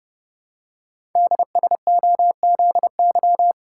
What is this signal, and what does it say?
Morse code, heard as a keyed tone, about 30 wpm, 690 Hz. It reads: DSOZY
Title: Code Groups - 5 Letters - 30wpm